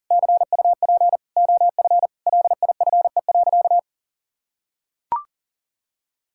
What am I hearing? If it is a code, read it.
Morse code, heard as a keyed tone, about 40 wpm, 700 Hz. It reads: CUP OF LIFE.
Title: Sentences from Top 1600 Words 40wpm